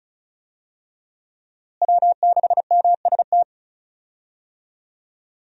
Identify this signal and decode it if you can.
Morse code, heard as a keyed tone, about 34 wpm, 700 Hz. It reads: W6MST
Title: USA Call Signs - 5 Characters - Repeated 3x - 35wpm